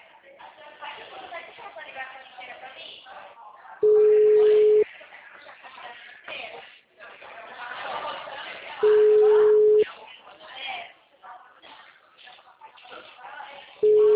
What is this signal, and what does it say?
Ringback tone (Europe, 425 Hz), with speech.